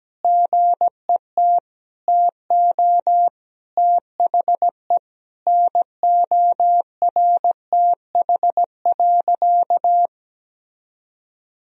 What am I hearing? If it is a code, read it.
Morse code, heard as a keyed tone, about 17 wpm, 700 Hz. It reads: GET TO THE NORTH.
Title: Sentences from Top 1000 Words - Review Part 1 - 17wpm